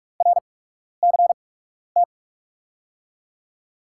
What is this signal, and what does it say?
Morse code, heard as a keyed tone, about 42 wpm, spaced as for 12 wpm, 700 Hz. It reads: RCT
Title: USA Call Signs - 6 Characters - 8x Character Spacing - 45wpm